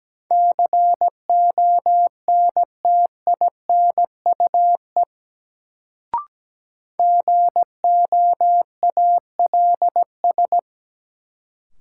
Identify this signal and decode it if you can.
Morse code, heard as a keyed tone, about 17 wpm, 700 Hz. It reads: CONTINUE GOALS
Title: Sentences from Top 1000 Words - Review Part 6 - New Word Drill - 17wpm